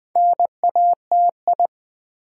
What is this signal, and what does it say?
Morse code, heard as a keyed tone, about 20 wpm, 700 Hz. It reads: NATI
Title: Build A Word - 2 Letters At A Time - 6-Letter Words - 20wpm